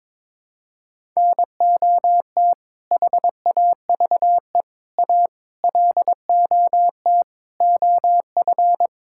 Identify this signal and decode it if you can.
Morse code, heard as a keyed tone, about 22 wpm, 700 Hz. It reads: NOT HAVE A LOT OF
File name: practice / 22wpm / Build A Phrase - 5-Word Phrases - 22wpm